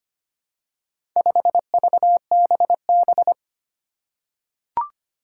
Morse code, heard as a keyed tone, about 25 wpm, 700 Hz. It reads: 5VBB
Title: Four Challenging Letter-Number Combinations 25wpm